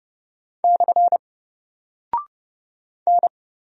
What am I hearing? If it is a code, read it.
Morse code, heard as a keyed tone, about 30 wpm, 700 Hz. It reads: / N
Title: Single Character (Letter:Number:Punctuation:Prosign) 30wpm